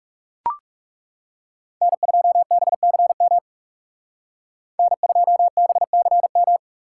Morse code, heard as a keyed tone, about 36 wpm, 700 Hz. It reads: N2BCM N2BCM
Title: Speed Racing - USA 1x3 Call Signs 30wpm